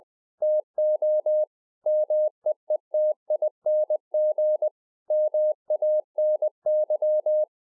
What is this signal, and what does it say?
Morse code, heard as a keyed tone, about 20 wpm, 610 Hz. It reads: TO MEETING MANY
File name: 2011-04-mp3 / KAZU JK1TCV 20WPM